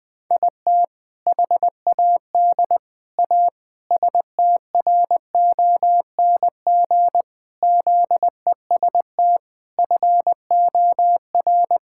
Morse code, heard as a keyed tone, about 21 wpm, 700 Hz. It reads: IT HAD A STRONG ZEST FOR
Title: All-Age Appropriate Jokes - 20wpm